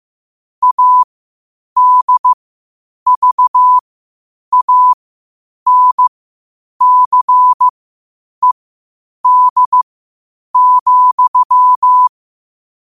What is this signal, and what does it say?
Morse code, heard as a keyed tone, about 15 wpm, spaced as for 8 wpm, 1000 Hz. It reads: ADVANCED,
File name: qotdfiles / 8wpm